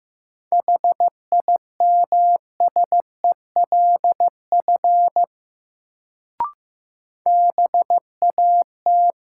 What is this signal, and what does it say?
Morse code, heard as a keyed tone, about 15 wpm, 700 Hz. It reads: HIMSELF BAT
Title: Sentences from Top 1500 Words - New Word Drill - 15wpm